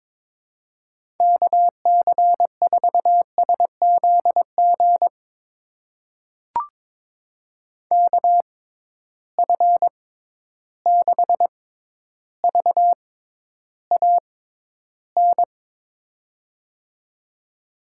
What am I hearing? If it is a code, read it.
Morse code, heard as a keyed tone, about 21 wpm, spaced as for 11 wpm, 700 Hz. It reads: KC4SZG KF6VAN
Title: USA Call Signs - 6 Characters - 6x Character Spacing - 22wpm